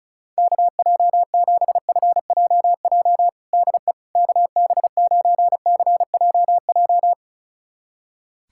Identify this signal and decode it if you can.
Morse code, heard as a keyed tone, about 36 wpm, 700 Hz. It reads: KJ7FJJ DE KB9CJJ
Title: USA Callsign DE Callsign - 6 Characters - 35wpm